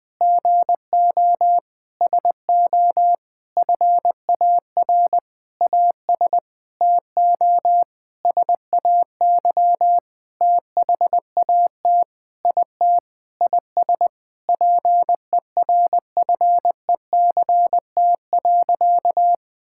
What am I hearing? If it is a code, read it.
Morse code, heard as a keyed tone, about 20 wpm, 700 Hz. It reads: GO SO FAR AS TO SAY THAT IT IS PERFECT.
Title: Sentences with 9-Word Phrases - 20wpm